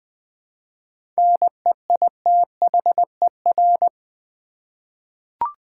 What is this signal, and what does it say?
Morse code, heard as a keyed tone, about 20 wpm, 700 Hz. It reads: NEITHER
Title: Seven Letter Words - 20wpm